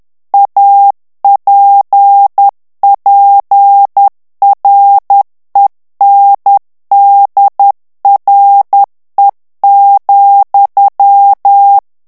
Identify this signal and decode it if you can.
Morse code, heard as a keyed tone, about 10 wpm, 800 Hz. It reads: APPRENDRE,
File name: morse mp3 / FRANCAIS TEXTE CLAIR 2 VITESSE 10 MOTS MINUTES